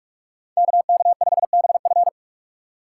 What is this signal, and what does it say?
Morse code, heard as a keyed tone, about 44 wpm, 690 Hz. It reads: KK5BF
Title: USA Call Signs - 5 Characters - 2x Character Spacing - 45wpm